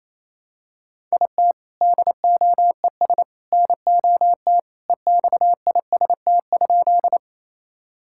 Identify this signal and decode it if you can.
Morse code, heard as a keyed tone, about 28 wpm, 700 Hz. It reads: IT DOES NOT EXIST?
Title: Sentences from Top 1000 Words - Review Part 5 - 28wpm